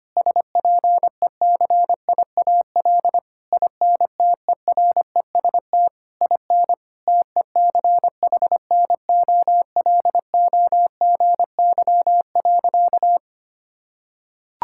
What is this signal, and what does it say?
Morse code, heard as a keyed tone, about 25 wpm, 700 Hz. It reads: SPECIAL INTEREST IN TECHNOLOGY.
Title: Sentences from Top 1000 Words - Review Part 5 - 25wpm